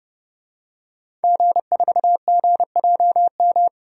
Morse code, heard as a keyed tone, about 30 wpm, 700 Hz. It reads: G4GJM
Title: DXCC-100 Call Signs - 5 Characters - Repeated 2x - 30wpm